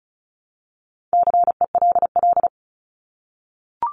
Morse code, heard as a keyed tone, about 34 wpm, 700 Hz. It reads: CELL
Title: ES:S Suffix - Top 100 Words with Base Words 35wpm